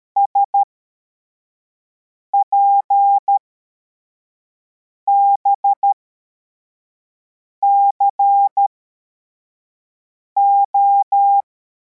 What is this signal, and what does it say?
Morse code, heard as a keyed tone, about 13 wpm, spaced as for 4 wpm, 800 Hz. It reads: SPBCO